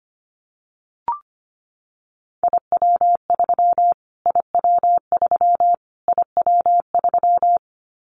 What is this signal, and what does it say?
Morse code, heard as a keyed tone, about 25 wpm, 700 Hz. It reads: IW3 IW3 IW3
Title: Callsign Prefixes for Top 25 DXCC - Repeated 3x - 25wpm